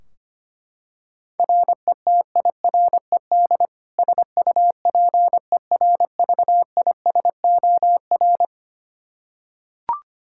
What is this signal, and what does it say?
Morse code, heard as a keyed tone, about 25 wpm, 700 Hz. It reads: RETIRED SUPERVISOR
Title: Practice QSO - Exchange 3 - More Info - 25wpm